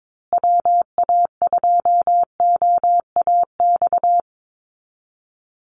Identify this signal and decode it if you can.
Morse code, heard as a keyed tone, about 22 wpm, 700 Hz. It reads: WA2OAX
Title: Call Signs Repeated 3x - 22wpm